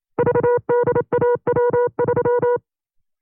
Morse code, heard as a keyed tone, about 27 wpm, 480 Hz. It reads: VDAW3
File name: CW-practice/25WPM